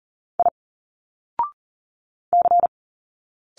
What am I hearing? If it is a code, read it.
Morse code, heard as a keyed tone, about 40 wpm, 700 Hz. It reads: I C
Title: Single Letters 40wpm